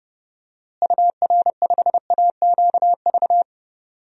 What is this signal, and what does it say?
Morse code, heard as a keyed tone, about 32 wpm, 700 Hz. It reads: UR5AQV
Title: DXCC-100 Call Signs - 6 Characters - Repeated 2x - 30wpm